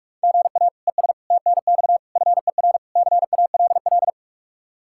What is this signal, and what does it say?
Morse code, heard as a keyed tone, about 59 wpm, 700 Hz. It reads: GA ES TNX FER CALL